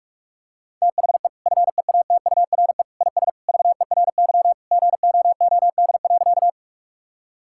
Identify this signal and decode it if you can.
Morse code, heard as a keyed tone, about 42 wpm, 700 Hz. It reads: THE FEATURE IS VERY GOOD.
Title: Sentences from Top 1000 Words - Review Part 5 - 45wpm